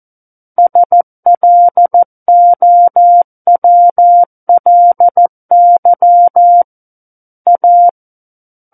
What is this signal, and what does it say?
Morse code, heard as a keyed tone, about 14 wpm, 700 Hz. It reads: SLOWLY A
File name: Book One / 05-Chapter 5 - 14 WPM